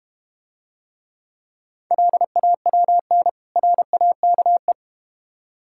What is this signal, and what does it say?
Morse code, heard as a keyed tone, about 31 wpm, 700 Hz. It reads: LAWN RAKE
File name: Easy Words - Speed-Racing with Hints - 25wpm